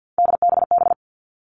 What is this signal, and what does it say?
Morse code, heard as a keyed tone, about 50 wpm, 700 Hz. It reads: DBB
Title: Three Challenging Letter-Number Combinations 50wpm